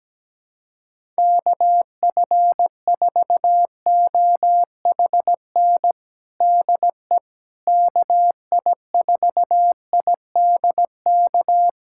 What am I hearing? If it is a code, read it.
Morse code, heard as a keyed tone, about 17 wpm, 700 Hz. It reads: KF4OHN DE KI4IDK
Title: USA Callsign DE Callsign - 6 Characters - 17wpm